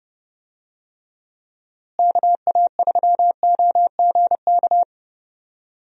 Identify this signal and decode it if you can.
Morse code, heard as a keyed tone, about 30 wpm, 700 Hz. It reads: KA3OGK
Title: USA Call Signs - 6 Characters - Repeated 3x - 30wpm